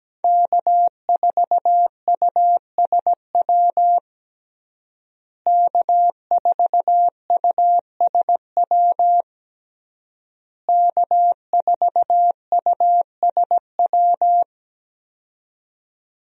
Morse code, heard as a keyed tone, about 17 wpm, 700 Hz. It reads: K4USW K4USW K4USW
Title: USA Call Signs - 5 Characters - Repeated 3x - 17wpm